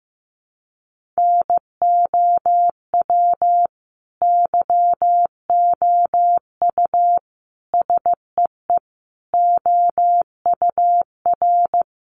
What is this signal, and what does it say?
Morse code, heard as a keyed tone, about 15 wpm, 700 Hz. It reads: NOW YOU SEE OUR
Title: Sentences from Top 1500 Words 15wpm